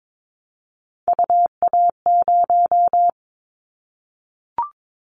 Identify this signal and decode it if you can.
Morse code, heard as a keyed tone, about 22 wpm, 700 Hz. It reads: UA0
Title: Callsign Prefixes for Top 25 DXCC - Repeated 3x - 22wpm